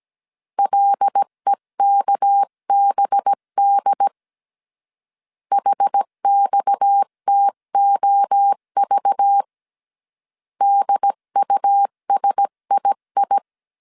Morse code, heard as a keyed tone, about 17 wpm, 800 Hz. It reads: LEKBD HXTOV DUSII
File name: CW / DET / 1 6 1500C S C 16wpm 2733